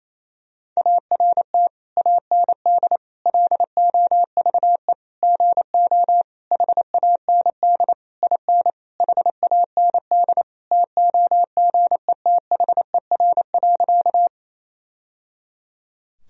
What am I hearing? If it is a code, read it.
Morse code, heard as a keyed tone, about 28 wpm, 700 Hz. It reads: ART AND LOVE GO HAND IN HAND TOGETHER.
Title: Sentences from Top 1000 Words - Review Part 3 - 28wpm